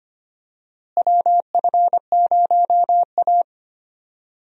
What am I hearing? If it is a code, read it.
Morse code, heard as a keyed tone, about 25 wpm, 700 Hz. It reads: WF0A
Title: USA Call Signs - 4 Characters - Repeated 2x - 25wpm